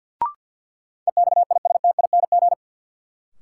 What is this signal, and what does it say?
Morse code, heard as a keyed tone, about 47 wpm, 700 Hz. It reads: EXISTING
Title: Sentences from Top 1500 Words - New Word Drill Alt - 50wpm